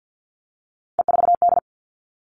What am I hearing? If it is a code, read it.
Morse code, heard as a keyed tone, about 50 wpm, 700 Hz. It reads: E4D
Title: All Cut Numbers - 3-Digit 50wpm